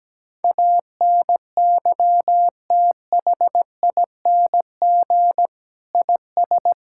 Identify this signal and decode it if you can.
Morse code, heard as a keyed tone, about 17 wpm, 700 Hz. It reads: ANYTHING IS
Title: Sentences from Top 1000 Words - Review Part 6 - 17wpm